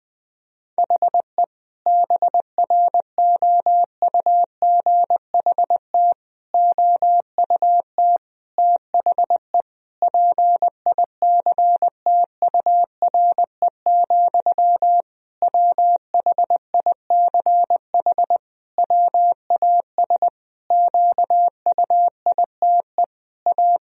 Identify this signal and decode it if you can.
Morse code, heard as a keyed tone, about 20 wpm, 700 Hz. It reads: HE BROUGHT OUT THE PICTURE, WHICH WAS QUITE A
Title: Sentences from Top 1000 Words - Review Part 3 - 20wpm